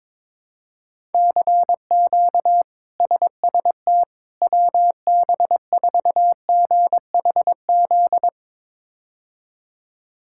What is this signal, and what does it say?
Morse code, heard as a keyed tone, about 22 wpm, 700 Hz. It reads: CQ SST WB4GHZ
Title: K1USN SST QSO Practice - 22wpm